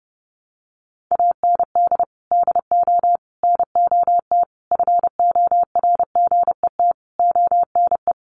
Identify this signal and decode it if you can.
Morse code, heard as a keyed tone, about 30 wpm, 700 Hz. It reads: AND DO NOT FORGET ONE
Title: Sentences from Top 1200 Words 30wpm